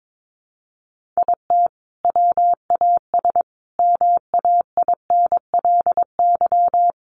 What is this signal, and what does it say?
Morse code, heard as a keyed tone, about 22 wpm, 700 Hz. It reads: IT WAS MAINLY
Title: Sentences from Top 1400 Words 22wpm